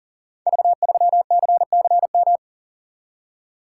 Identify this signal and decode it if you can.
Morse code, heard as a keyed tone, about 42 wpm, 700 Hz. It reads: V3CCM
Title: Code Groups - 5 Letter:Numbers - 40wpm